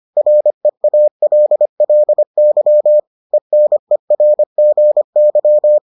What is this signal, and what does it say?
Morse code, heard as a keyed tone, about 25 wpm, 600 Hz. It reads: REALLY ENERGY